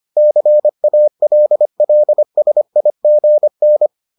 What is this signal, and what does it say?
Morse code, heard as a keyed tone, about 25 wpm, 600 Hz. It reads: CALLSIGN